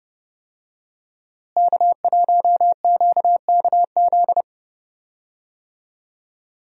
Morse code, heard as a keyed tone, about 32 wpm, 700 Hz. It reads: K1QKZ
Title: USA Call Signs - 5 Characters - Repeated 2x - 30wpm